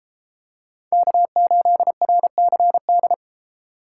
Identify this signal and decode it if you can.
Morse code, heard as a keyed tone, about 31 wpm, 700 Hz. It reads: K8RCD